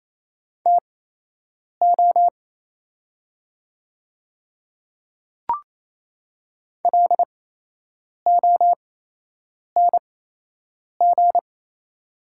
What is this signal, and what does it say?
Morse code, heard as a keyed tone, about 28 wpm, spaced as for 8 wpm, 700 Hz. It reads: TO LONG
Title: 8x Character Spacing - Top 100 Words - 28wpm